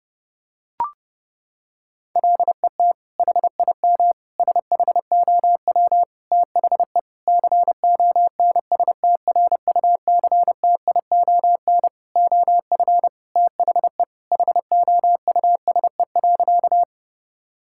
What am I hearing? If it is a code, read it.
Morse code, heard as a keyed tone, about 30 wpm, 700 Hz. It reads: LET HIM SHOW THE CONSTRUCTION OF THE HOUSE.
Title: Sentences from Top 1600 Words 30wpm